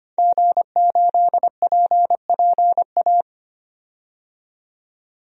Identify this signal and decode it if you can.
Morse code, heard as a keyed tone, about 25 wpm, 700 Hz. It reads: G8PPA
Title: DXCC-100 Call Signs - 5 Characters - 25wpm